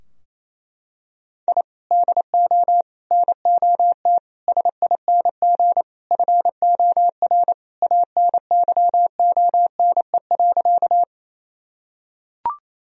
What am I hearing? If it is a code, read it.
Morse code, heard as a keyed tone, about 28 wpm, 700 Hz. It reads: I DO NOT SING FOR ANYONE.
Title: Sentences from Top 1000 Words - Review Part 2 - 28wpm